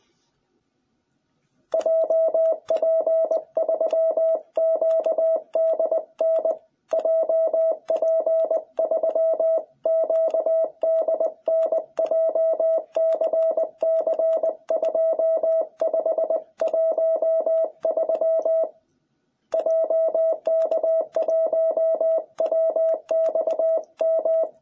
Morse code, heard as a keyed tone, about 20 wpm, 660 Hz. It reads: JP3QDN JP3QDN JCC2513 JK1WXM